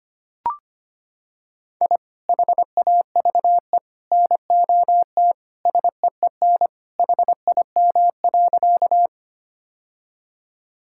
Morse code, heard as a keyed tone, about 25 wpm, 700 Hz. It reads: I HAVE NOT SEEN HIM.
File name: practice / 25wpm / Sentences with 3-Word Phrases - 25wpm